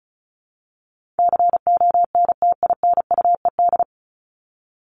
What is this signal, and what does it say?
Morse code, heard as a keyed tone, about 34 wpm, 700 Hz. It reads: CONTINUED
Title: ED Suffix - Top 100 Words with Base Words 35wpm